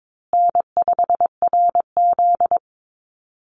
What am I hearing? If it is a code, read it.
Morse code, heard as a keyed tone, about 22 wpm, 700 Hz. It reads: N5RZ
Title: Easy Call Signs Repeated 3x - 22wpm